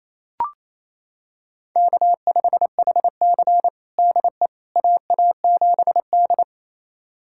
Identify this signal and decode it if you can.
Morse code, heard as a keyed tone, about 28 wpm, 700 Hz. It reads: K5HC DE AA7D
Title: USA Callsign DE Callsign - 4 Characters - 28wpm